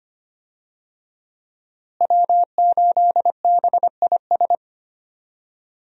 Morse code, heard as a keyed tone, about 25 wpm, 700 Hz. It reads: W8BIS
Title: USA Call Signs - 5 Characters - Repeated 3x - 25wpm